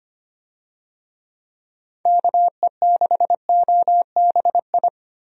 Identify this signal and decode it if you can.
Morse code, heard as a keyed tone, about 25 wpm, 700 Hz. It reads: KE6OBI
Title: USA Call Signs - 6 Characters - Repeated 3x - 25wpm